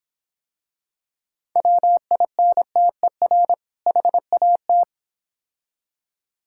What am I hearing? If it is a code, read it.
Morse code, heard as a keyed tone, about 26 wpm, 700 Hz. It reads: WINTER HAT